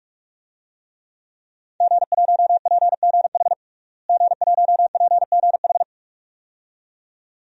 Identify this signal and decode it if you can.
Morse code, heard as a keyed tone, about 44 wpm, 700 Hz. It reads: G1PGH G1PGH
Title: DXCC-100 Call Signs - 5 Characters - Repeated 2x - 45wpm